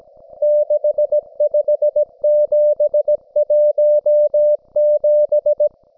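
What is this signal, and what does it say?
Morse code, heard as a keyed tone, about 17 wpm, 600 Hz. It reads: 65717
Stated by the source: Russian military Morse station, M32, RDL, on 5268 kHz.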